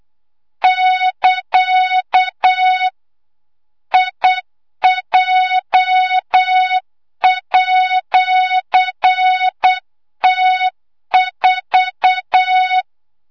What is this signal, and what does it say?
Morse code, heard as a keyed tone, about 8 wpm, 3700 Hz, with noise. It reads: <KA> IJ@T4
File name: Morze-HA2MN / hangp / 2 06 GYAKORLAT